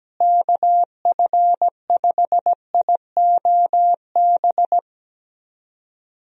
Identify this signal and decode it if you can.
Morse code, heard as a keyed tone, about 17 wpm, 700 Hz. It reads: KF5IOB
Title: USA Call Signs - 6 Characters - Repeated 2x - 17wpm